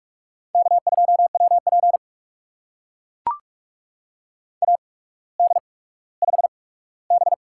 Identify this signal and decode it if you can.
Morse code, heard as a keyed tone, about 44 wpm, spaced as for 19 wpm, 690 Hz. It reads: K2WP AD5B
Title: USA Call Signs - 4 Characters - 8x Character Spacing - 45wpm